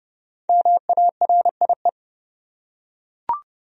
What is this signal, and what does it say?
Morse code, heard as a keyed tone, about 30 wpm, 700 Hz. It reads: MARIE
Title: USA Common Names - Max 5 Letters - 30wpm